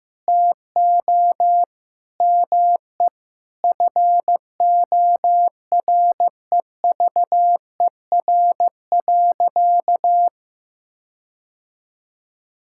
Morse code, heard as a keyed tone, about 15 wpm, 700 Hz. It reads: TO ME FOREVER.
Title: Sentences from Top 1600 Words 15wpm